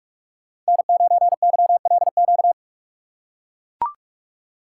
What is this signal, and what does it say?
Morse code, heard as a keyed tone, about 43 wpm, 700 Hz. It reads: N9YLQ
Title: USA Call Signs - 5 Characters - 8x Character Spacing - 45wpm